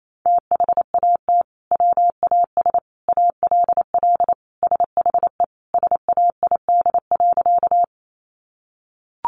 Morse code, heard as a keyed tone, about 28 wpm, 700 Hz. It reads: THAT WAS ALL SHE SAID.
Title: Sentences from Top 100 Words 28wpm